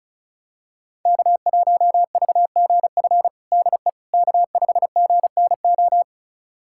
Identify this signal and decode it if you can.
Morse code, heard as a keyed tone, about 36 wpm, 700 Hz. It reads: K1VGF DE K5GNO
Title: USA Callsign DE Callsign - 5 Characters - 35wpm